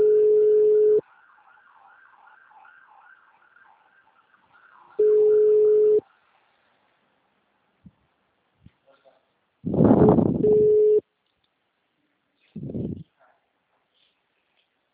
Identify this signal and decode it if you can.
Ringback tone (Europe, 425 Hz), with speech.
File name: out- -2055- - - .46037